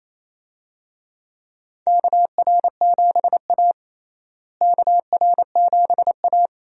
Morse code, heard as a keyed tone, about 28 wpm, 700 Hz. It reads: KR7A KR7A
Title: USA Call Signs - 4 Characters - Repeated 2x - 28wpm